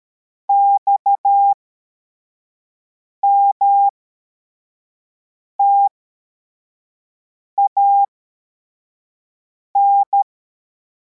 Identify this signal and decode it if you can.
Morse code, heard as a keyed tone, about 13 wpm, spaced as for 4 wpm, 800 Hz. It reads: XMTAN